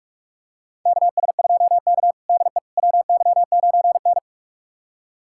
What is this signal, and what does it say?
Morse code, heard as a keyed tone, about 46 wpm, 700 Hz. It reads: KS2K DE WY9N
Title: USA Callsign DE Callsign - 4 Characters - 45wpm